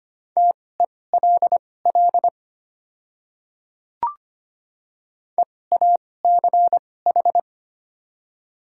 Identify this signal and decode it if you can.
Morse code, heard as a keyed tone, about 25 wpm, spaced as for 18 wpm, 700 Hz. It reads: TELL EACH
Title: 2x Character Spacing - Top 200 Words - 25wpm